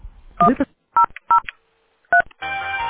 DTMF, heard as telephone touch tones, tones of about 100 ms. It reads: 1003